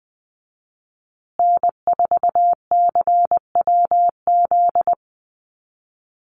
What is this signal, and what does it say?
Morse code, heard as a keyed tone, about 20 wpm, 700 Hz. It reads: N4CWZ
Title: USA Call Signs Repeated 3x 20wpm